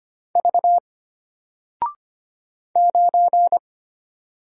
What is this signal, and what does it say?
Morse code, heard as a keyed tone, about 25 wpm, 700 Hz. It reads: V 9
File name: Single-Letter-Number-25wpm